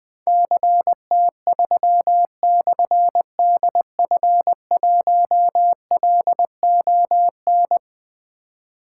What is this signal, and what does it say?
Morse code, heard as a keyed tone, about 20 wpm, 700 Hz. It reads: CT3/DF1LON
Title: Difficult Call Signs 20wpm